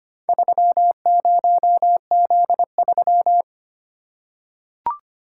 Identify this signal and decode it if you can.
Morse code, heard as a keyed tone, about 25 wpm, 700 Hz. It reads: 30Z3
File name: practice / 25wpm / Four Letter-Number Combinations - 25wpm